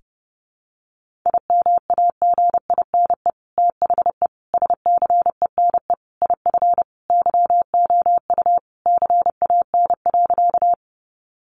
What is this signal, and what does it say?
Morse code, heard as a keyed tone, about 30 wpm, 700 Hz. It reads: IMAGINE THE SCENE IF YOU CAN.
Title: Sentences from Top 1400 Words 30wpm